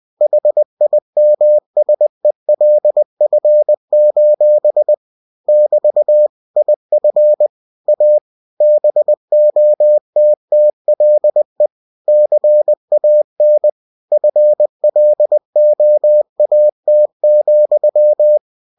Morse code, heard as a keyed tone, about 20 wpm, 600 Hz. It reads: HIMSELF: <BT> IF A BOTTLE CAN FLOAT,
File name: cw-books / winnie-the-pooh / winnie-20wpm-page-0095